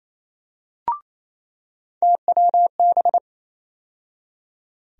Morse code, heard as a keyed tone, about 28 wpm, 700 Hz. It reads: TWB
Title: USA Call Sign Postfixes - 3 Letters 28wpm